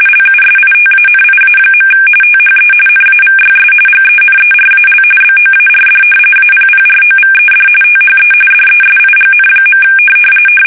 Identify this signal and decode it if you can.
RTTY, about 45.45 baud. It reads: LUSITANIA. THE NAVYS GENERAL BOARD APPROVED THE DESIGN.
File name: fsk850 / battleshipnj